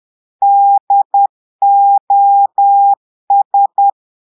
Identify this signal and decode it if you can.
Morse code, heard as a keyed tone, about 10 wpm, 800 Hz. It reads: DOS
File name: Portugais - texte en clair 1 - vitesse - 10 mots minute